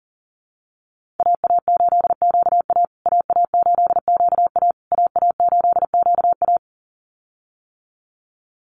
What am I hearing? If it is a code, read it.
Morse code, heard as a keyed tone, about 40 wpm, 700 Hz. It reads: AA8QA AA8QA AA8QA
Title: USA Call Signs Repeated 3x 40wpm